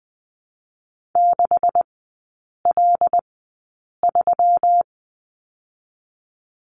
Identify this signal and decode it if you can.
Morse code, heard as a keyed tone, about 20 wpm, spaced as for 8 wpm, 700 Hz. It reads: 6L3
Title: 127 Lesson 21e - Number 6 - Three Characters 20wpm